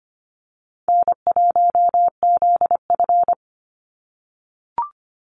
Morse code, heard as a keyed tone, about 25 wpm, 700 Hz. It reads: N1ZF
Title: NA SOTA Chaser Callsigns 25wpm